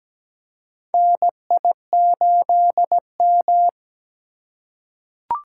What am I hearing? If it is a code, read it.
Morse code, heard as a keyed tone, about 17 wpm, 700 Hz. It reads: NI8M
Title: USA Call Signs - 4 Characters - 3x Character Spacing - 17wpm